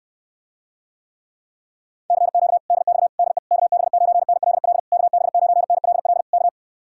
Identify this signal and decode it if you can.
Morse code, heard as a keyed tone, about 69 wpm, 700 Hz. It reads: CQ DX DE KB9NLL KB9NLL K